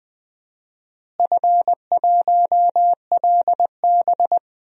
Morse code, heard as a keyed tone, about 20 wpm, 700 Hz. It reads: F1LB
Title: DXCC-100 Call Signs - 4 Characters - Repeated 2x - 20wpm